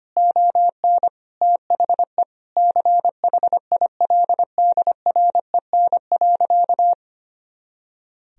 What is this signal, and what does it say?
Morse code, heard as a keyed tone, about 25 wpm, 700 Hz. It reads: ON THE CHILDREN.
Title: Sentences from Top 300 Words and Abbreviations 25wpm